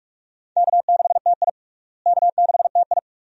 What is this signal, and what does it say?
Morse code, heard as a keyed tone, about 43 wpm, 700 Hz. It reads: K6TI K6TI
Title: USA Call Signs - 4 Characters - Repeated 2x - 45wpm